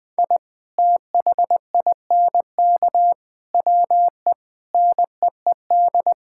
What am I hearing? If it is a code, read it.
Morse code, heard as a keyed tone, about 20 wpm, 700 Hz. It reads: I THINK WE NEED
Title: Sentences with 5-Word Phrases - Spoken Beginning - 20wpm